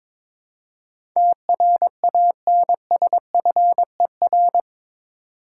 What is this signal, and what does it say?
Morse code, heard as a keyed tone, about 22 wpm, 700 Hz. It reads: TRANSFER
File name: practice / 22wpm / Eight Letter Words - 22wpm